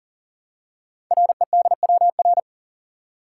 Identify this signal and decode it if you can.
Morse code, heard as a keyed tone, about 40 wpm, 700 Hz. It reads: REDWR
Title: Code Groups - 5 Letters - 40wpm